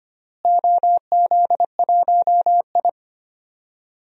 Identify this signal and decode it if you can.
Morse code, heard as a keyed tone, about 25 wpm, 700 Hz. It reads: OZ1I
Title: DXCC-100 Call Signs - 4 Characters - Repeated 2x - 25wpm